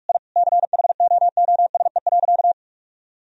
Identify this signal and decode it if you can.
Morse code, heard as a keyed tone, about 46 wpm, 700 Hz. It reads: I CHOOSE.
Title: Sentences from Top 1000 Words - Review Part 4 - 45wpm